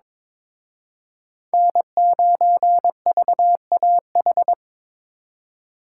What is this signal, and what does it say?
Morse code, heard as a keyed tone, about 26 wpm, 700 Hz. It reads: N9VAH
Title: Speed Racing - USA Call Signs 22wpm